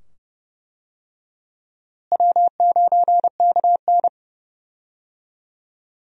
Morse code, heard as a keyed tone, about 30 wpm, 700 Hz. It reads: W9KN
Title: USA Call Signs - 4 Characters - Repeated 2x - 30wpm